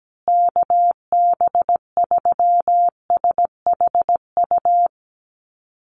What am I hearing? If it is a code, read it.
Morse code, heard as a keyed tone, about 17 wpm, 700 Hz. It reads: KB3SHU
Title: USA Call Signs Repeated 3x 17wpm